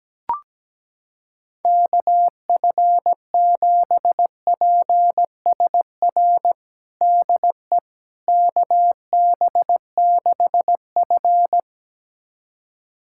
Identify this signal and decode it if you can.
Morse code, heard as a keyed tone, about 17 wpm, 700 Hz. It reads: KF7PSR DE KB6F
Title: USA Callsign DE Callsign 17wpm